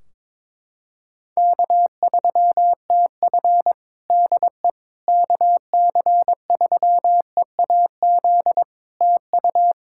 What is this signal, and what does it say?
Morse code, heard as a keyed tone, about 22 wpm, 700 Hz. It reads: K3TF DE KC3EAZ TU
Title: Practice QSO - Exchange 4 - Ending - 22wpm